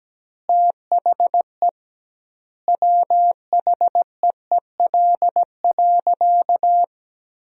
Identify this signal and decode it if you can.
Morse code, heard as a keyed tone, about 17 wpm, 700 Hz. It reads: THE WHEEL.
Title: 2x Word-Spacing - Sentences from Top 500 Words - 17wpm